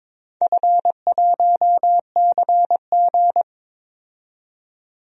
Morse code, heard as a keyed tone, about 22 wpm, 700 Hz. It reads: F1CG
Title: DXCC-100 Call Signs - 4 Characters - 22wpm